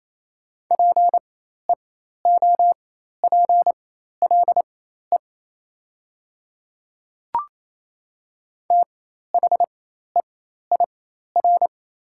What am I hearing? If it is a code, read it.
Morse code, heard as a keyed tone, about 28 wpm, spaced as for 13 wpm, 700 Hz. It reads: PEOPLE THEIR
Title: 4x Character Spacing - Top 200 Words - 28wpm